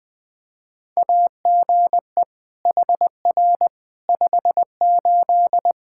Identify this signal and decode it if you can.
Morse code, heard as a keyed tone, about 20 wpm, 700 Hz. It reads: AGE HR 58
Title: Practice QSO - Exchange 3 - More Info - 20wpm